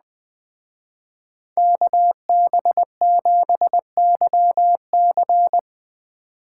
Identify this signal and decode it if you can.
Morse code, heard as a keyed tone, about 20 wpm, 700 Hz. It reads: KB7YC
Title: Speed Racing - USA 2x2 Call Signs 17wpm